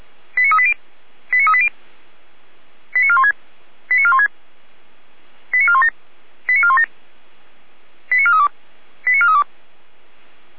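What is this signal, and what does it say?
Five-tone selective call (ZVEI), heard as quick runs of five tones, tones of about 70 ms. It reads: 89380 89380 89416 89416 89417 89417 89432 89432
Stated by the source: Received on SDR as a ZVEI five-tone selective call.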